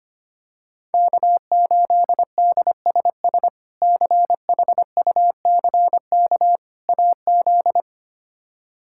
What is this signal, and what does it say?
Morse code, heard as a keyed tone, about 25 wpm, 700 Hz. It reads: K8DSS CHUCK AZ
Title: K1USN SST QSO Practice - 25wpm